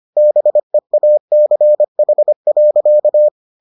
Morse code, heard as a keyed tone, about 25 wpm, 600 Hz. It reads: BEACH.